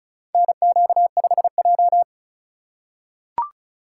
Morse code, heard as a keyed tone, about 36 wpm, 700 Hz. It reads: NQ5J
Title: USA Call Signs - 2x Character Spacing - 35wpm